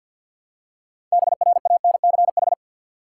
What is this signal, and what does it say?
Morse code, heard as a keyed tone, about 47 wpm, 700 Hz. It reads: BRANCH
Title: Six Letter Words - 50wpm